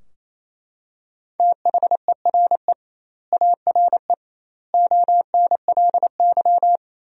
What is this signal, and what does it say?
Morse code, heard as a keyed tone, about 28 wpm, 700 Hz. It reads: THERE ARE ONLY
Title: 2x Word-Spacing - Sentences from Top 200 Words - 28wpm